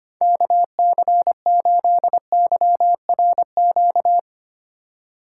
Morse code, heard as a keyed tone, about 32 wpm, 700 Hz. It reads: KC8YRQ